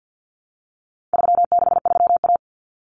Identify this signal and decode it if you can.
Morse code, heard as a keyed tone, about 50 wpm, 700 Hz. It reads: 363A
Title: Common Cut Numbers - 4-Digit 50wpm